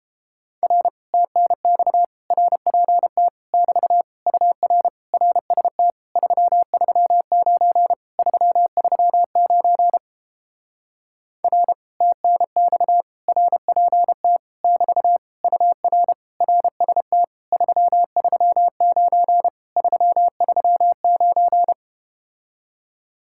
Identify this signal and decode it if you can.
Morse code, heard as a keyed tone, about 32 wpm, 700 Hz. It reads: R TNX RPT <BT> UR RST 339 339 R TNX RPT <BT> UR RST 339 339